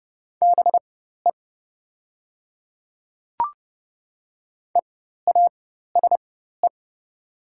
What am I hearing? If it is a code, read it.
Morse code, heard as a keyed tone, about 30 wpm, spaced as for 14 wpm, 700 Hz. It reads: BE EASE
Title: 4x Character Spacing - Top 300 Words - 30wpm